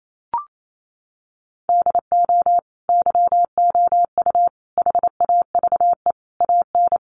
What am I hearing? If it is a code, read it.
Morse code, heard as a keyed tone, about 28 wpm, 700 Hz. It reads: DO YOU HAVE AN
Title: Sentences from Top 1500 Words 28wpm